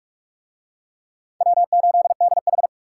Morse code, heard as a keyed tone, about 41 wpm, 700 Hz. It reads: W8DH